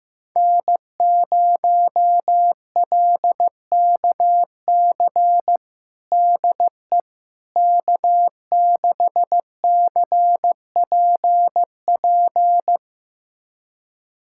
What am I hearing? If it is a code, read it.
Morse code, heard as a keyed tone, about 15 wpm, 700 Hz. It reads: N0LKC DE K6CPP
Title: USA Callsign DE Callsign - 5 Characters - 15wpm